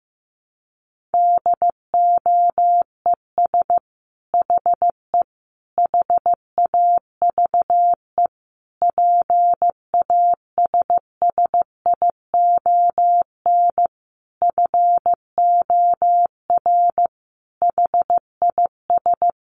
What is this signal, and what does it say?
Morse code, heard as a keyed tone, about 15 wpm, 700 Hz. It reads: DOES HE HAVE PASSION FOR HIS
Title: Sentences from Top 1300 Words 15wpm